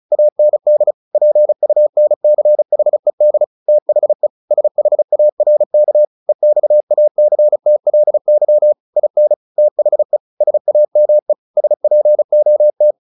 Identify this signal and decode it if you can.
Morse code, heard as a keyed tone, about 34 wpm, 600 Hz. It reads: AND PUNCHED THE SHARK EXACTLY IN THE SAME SPOT
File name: the old man and the sea 2 35wpm